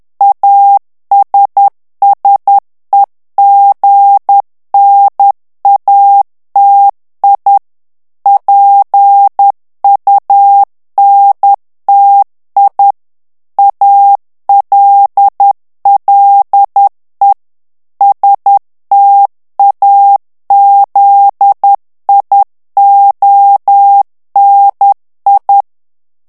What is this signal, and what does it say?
Morse code, heard as a keyed tone, about 10 wpm, 800 Hz. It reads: ASSEGNATI PUNTI ALLE STAZIONI
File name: ITALIEN texte clair 2 vitesse 10 mots minute